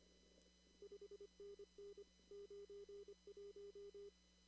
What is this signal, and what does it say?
Morse code, heard as a keyed tone, about 25 wpm, 400 Hz. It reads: 5NN 91
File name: CW / kt5j wpx cw-op-k5pi